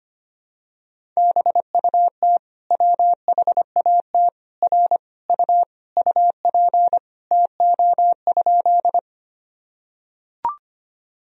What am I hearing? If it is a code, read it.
Morse code, heard as a keyed tone, about 25 wpm, 700 Hz. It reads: BUT WHAT R U UP TO?
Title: Sentences from Top 100 Words and Abbreviations 25wpm